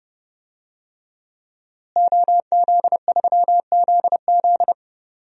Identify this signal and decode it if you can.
Morse code, heard as a keyed tone, about 30 wpm, 700 Hz. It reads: OZ3ZZ
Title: DXCC-100 Call Signs - 5 Characters - Repeated 3x - 30wpm